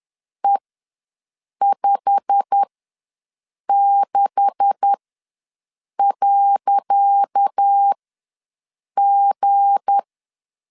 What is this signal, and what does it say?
Morse code, heard as a keyed tone, about 10 wpm, spaced as for 6 wpm, 800 Hz. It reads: E56.G
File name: LEZIONI / 10 SIGNS / 10C DOT ws 10wpm 4213